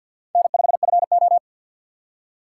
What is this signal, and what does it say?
Morse code, heard as a keyed tone, about 52 wpm, 700 Hz. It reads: N5FO
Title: USA Call Signs - 4 Characters - 6x Character Spacing - 50wpm